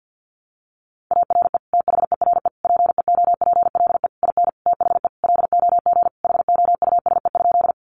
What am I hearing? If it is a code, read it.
Morse code, heard as a keyed tone, about 50 wpm, 700 Hz. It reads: ARE THERE PEOPLE IN THE LOG HOUSE?
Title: Sentences from Top 1400 Words 50wpm